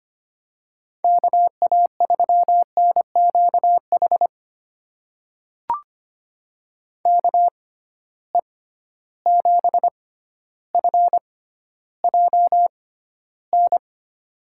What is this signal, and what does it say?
Morse code, heard as a keyed tone, about 25 wpm, spaced as for 13 wpm, 700 Hz. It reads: KA3NQH KE7FJN
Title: USA Call Signs - 6 Characters - 6x Character Spacing - 25wpm